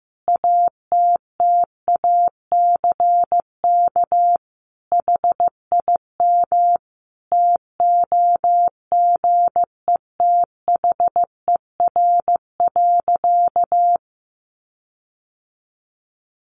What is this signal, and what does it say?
Morse code, heard as a keyed tone, about 15 wpm, 700 Hz. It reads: ATTACK HIM TOGETHER.
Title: Sentences from Top 1500 Words 15wpm